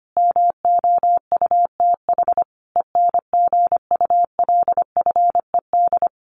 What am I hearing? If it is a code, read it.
Morse code, heard as a keyed tone, about 25 wpm, 700 Hz. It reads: MOUTH ENGULFED
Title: A Sound Of Thunder 25wpm